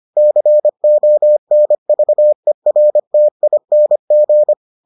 Morse code, heard as a keyed tone, about 25 wpm, 600 Hz. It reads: CONVERTING